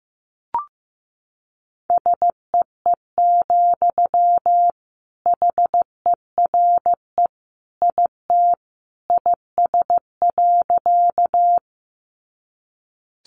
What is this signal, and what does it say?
Morse code, heard as a keyed tone, about 15 wpm, 700 Hz. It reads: SEE, HERE IT IS.
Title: Sentences from Top 200 Words 15wpm